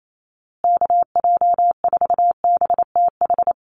Morse code, heard as a keyed tone, about 28 wpm, 690 Hz. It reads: KJ4BTH
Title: USA Call Signs Repeated 3x 28wpm